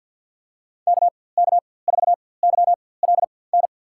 Morse code, heard as a keyed tone, about 47 wpm, spaced as for 23 wpm, 700 Hz. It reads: KK4YLN
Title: USA Call Signs - 6 Characters - 4x Character Spacing - 50wpm